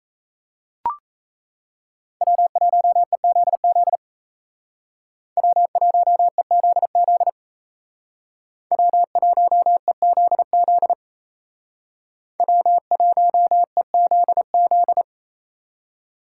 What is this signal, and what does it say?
Morse code, heard as a keyed tone, about 33 wpm, 700 Hz. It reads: W1EZZ W1EZZ W1EZZ W1EZZ
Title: Speed Racing - USA 1x3 Call Signs 28wpm